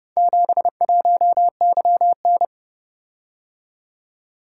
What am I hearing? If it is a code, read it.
Morse code, heard as a keyed tone, about 30 wpm, 700 Hz. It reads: 71YN